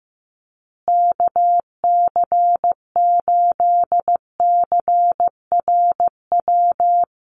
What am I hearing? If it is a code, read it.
Morse code, heard as a keyed tone, about 15 wpm, 700 Hz. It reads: KC8CRW
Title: USA Call Signs Repeated 2x 15wpm